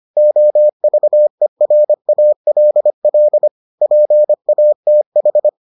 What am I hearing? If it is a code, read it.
Morse code, heard as a keyed tone, about 25 wpm, 600 Hz. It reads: OVERALL PATH